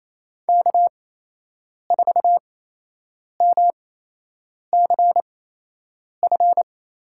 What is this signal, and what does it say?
Morse code, heard as a keyed tone, about 28 wpm, spaced as for 8 wpm, 700 Hz. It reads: K4MCF
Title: USA Call Signs - 5 Characters - 8x Character Spacing - 28wpm